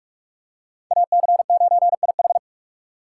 Morse code, heard as a keyed tone, about 47 wpm, 700 Hz. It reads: AC9IH
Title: USA Call Signs - 5 Characters - 4x Character Spacing - 45wpm